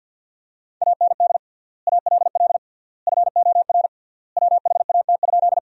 Morse code, heard as a keyed tone, about 50 wpm, 700 Hz. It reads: AND ALL FOR WHAT?
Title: 3x Word-Spacing - Sentences from Top 100 Words - 50wpm